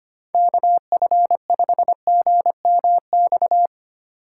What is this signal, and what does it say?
Morse code, heard as a keyed tone, about 25 wpm, 700 Hz. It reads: KF5GMX
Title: USA Call Signs - 6 Characters - 3x Character Spacing - 25wpm